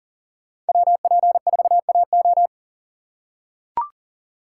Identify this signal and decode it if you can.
Morse code, heard as a keyed tone, about 39 wpm, 700 Hz. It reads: WP4AO
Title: USA Call Signs - 5 Characters - 4x Character Spacing - 40wpm